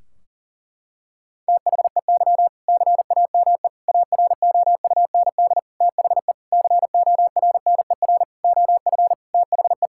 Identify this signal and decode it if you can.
Morse code, heard as a keyed tone, about 37 wpm, 700 Hz. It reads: THEY CAME AROUND THE CORNER OF THE
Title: Sentences from Top 1600 Words 40wpm